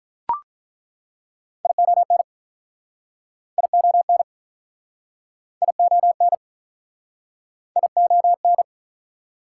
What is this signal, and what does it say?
Morse code, heard as a keyed tone, about 48 wpm, 700 Hz. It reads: ION ION ION ION